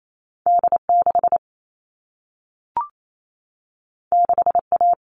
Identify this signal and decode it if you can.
Morse code, heard as a keyed tone, about 28 wpm, 700 Hz. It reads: D6 6A
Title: All Cut Numbers - 2-Digit 28wpm